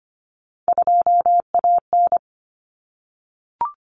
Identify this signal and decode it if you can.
Morse code, heard as a keyed tone, about 25 wpm, 700 Hz. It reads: 2AN
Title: Common Cut Numbers - 3-Digit 25wpm